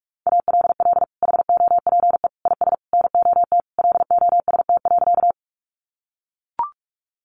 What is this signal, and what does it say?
Morse code, heard as a keyed tone, about 44 wpm, 700 Hz. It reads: ALL HOPE IS NOT LOST.
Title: Sentences from Top 1400 Words 45wpm